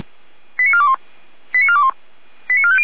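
Five-tone selective call (ZVEI), heard as quick runs of five tones, tones of about 70 ms. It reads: 89421 89421 89450
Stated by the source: ZVEI five-tone call received by SDR.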